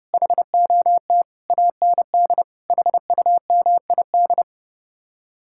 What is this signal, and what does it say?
Morse code, heard as a keyed tone, about 40 wpm, 700 Hz. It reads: HOT AND HUMID